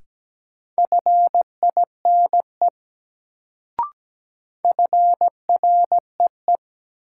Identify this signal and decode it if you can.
Morse code, heard as a keyed tone, about 17 wpm, 700 Hz. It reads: FINE FREE
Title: Sentences from Top 500 Words - New Word Drill - 17wpm